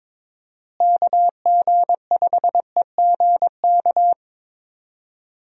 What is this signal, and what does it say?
Morse code, heard as a keyed tone, about 22 wpm, 700 Hz. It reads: KG5EGK
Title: USA Call Signs - 6 Characters - Repeated 2x - 22wpm